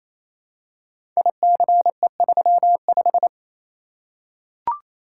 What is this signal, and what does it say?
Morse code, heard as a keyed tone, about 27 wpm, 700 Hz. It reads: ICE35
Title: Code Groups - 5 Letter:Numbers - 28wpm